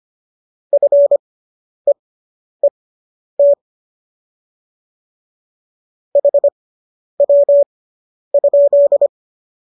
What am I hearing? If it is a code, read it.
Morse code, heard as a keyed tone, about 25 wpm, spaced as for 10 wpm, 570 Hz. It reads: FEET HW?